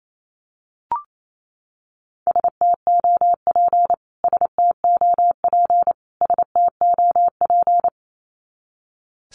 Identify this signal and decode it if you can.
Morse code, heard as a keyed tone, about 28 wpm, 700 Hz. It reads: STOP STOP STOP
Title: Top 300 Words Repeated 28wpm